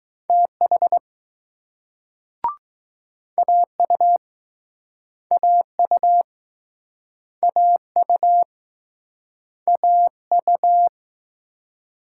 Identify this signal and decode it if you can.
Morse code, heard as a keyed tone, about 21 wpm, 700 Hz. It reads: TH AU AU AU AU